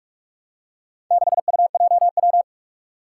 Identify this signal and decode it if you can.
Morse code, heard as a keyed tone, about 44 wpm, 700 Hz. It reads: 6UJW
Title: Four Challenging Letter-Number Combinations 45wpm